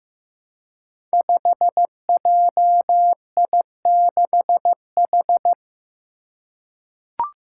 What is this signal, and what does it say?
Morse code, heard as a keyed tone, about 15 wpm, 700 Hz. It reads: 5JI6H
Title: Five Challenging Letter-Number Combinations 15wpm